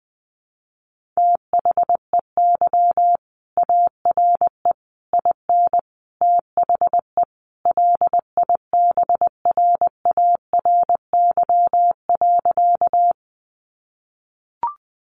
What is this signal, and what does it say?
Morse code, heard as a keyed tone, about 20 wpm, 700 Hz. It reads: THEY ARE IN THE LIBRARY.
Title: Sentences from Top 600 Words 20wpm